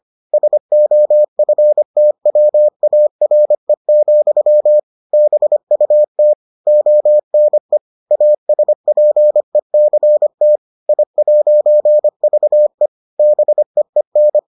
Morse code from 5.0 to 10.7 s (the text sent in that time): BUT ONE ASPECT